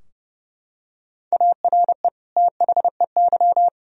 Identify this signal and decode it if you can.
Morse code, heard as a keyed tone, about 32 wpm, 700 Hz. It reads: ARE THEY
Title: Sentences from Top 1000 Words - Review Part 4 - 30wpm